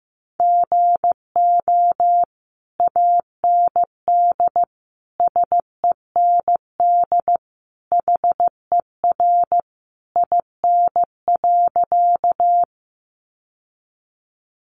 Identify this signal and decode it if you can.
Morse code, heard as a keyed tone, about 15 wpm, 700 Hz. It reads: GO AND SEND HER IN.
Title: Sentences from Top 1500 Words 15wpm